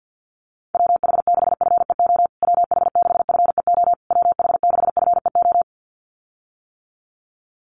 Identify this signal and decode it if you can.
Morse code, heard as a keyed tone, about 50 wpm, 700 Hz. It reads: WH6FEO WH6FEO WH6FEO
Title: USA Call Signs Repeated 3x 50wpm